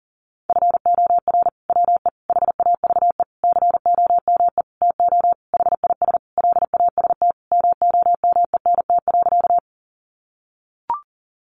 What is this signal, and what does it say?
Morse code, heard as a keyed tone, about 40 wpm, 700 Hz. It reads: FOR WE HAVE COME TO HIS LAST MOMENT.
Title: Sentences from Top 1000 Words 40wpm